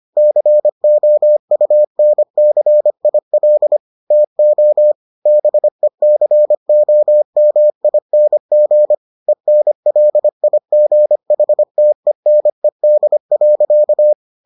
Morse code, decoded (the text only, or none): COUNCIL TO BECOMING ENLIGHTENED.